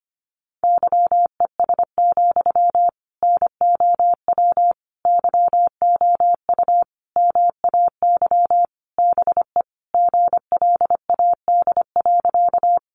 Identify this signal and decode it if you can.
Morse code, heard as a keyed tone, about 25 wpm, 700 Hz. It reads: YES, NOW YOU MAY BE GLAD.
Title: Sentences from Top 800 Words 25wpm